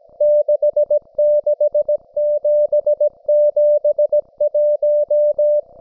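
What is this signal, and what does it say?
Morse code, heard as a keyed tone, about 17 wpm, 590 Hz. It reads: 66771
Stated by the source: Russian military Morse station, M32, RDL, on 5268 kHz.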